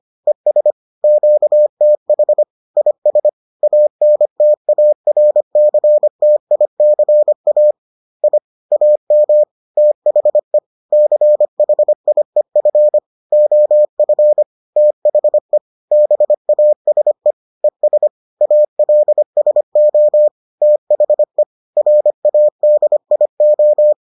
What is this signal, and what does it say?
Morse code, heard as a keyed tone, about 25 wpm, 600 Hz. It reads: ES QTH IS ANTARCTICA I AM THE CHIEF OF THE BASE ES ALSO THE RADIO